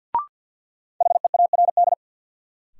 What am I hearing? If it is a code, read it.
Morse code, heard as a keyed tone, about 53 wpm, 700 Hz. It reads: HEARD
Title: Sentences from Top 400 Words - New Word Drill - 50wpm